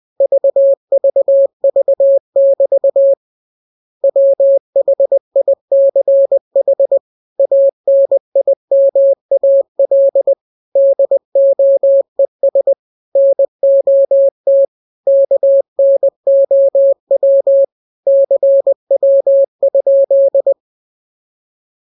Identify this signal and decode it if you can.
Morse code, heard as a keyed tone, about 20 wpm, 550 Hz. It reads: VVV<BT> WHICH ANIMAL DOES NOT KNOW CW?
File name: jokes / joke 26 20 0000